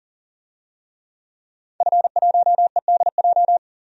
Morse code, heard as a keyed tone, about 37 wpm, 700 Hz. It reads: F1EDJ
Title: DXCC-100 Call Signs - 5 Characters - 40wpm